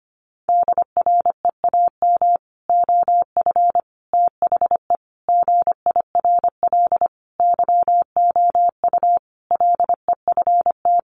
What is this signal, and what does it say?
Morse code, heard as a keyed tone, about 25 wpm, 700 Hz. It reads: DREAM OF THE GIRL YOU LEFT
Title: Sentences from Top 1500 Words 25wpm